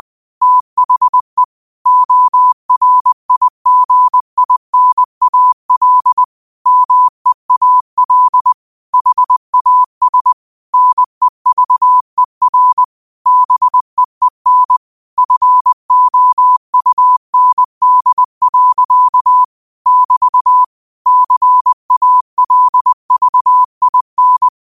Morse code, heard as a keyed tone, about 20 wpm, 1000 Hz. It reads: THE ORIGINAL MEAL HAS NEVER BEEN FOUND. <BT> CALVIN